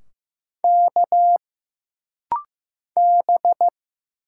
Morse code, heard as a keyed tone, about 15 wpm, 700 Hz. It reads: K B
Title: Single Letters - Rapid-Fire 15wpm